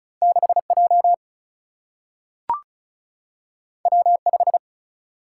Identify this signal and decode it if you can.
Morse code, heard as a keyed tone, about 36 wpm, 700 Hz. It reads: 6 J W 5
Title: Single-Letter Single-Number 35wpm